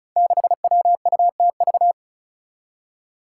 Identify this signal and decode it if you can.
Morse code, heard as a keyed tone, about 36 wpm, 700 Hz. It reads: 6WUTV
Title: Five Challenging Letter-Number Combinations 35wpm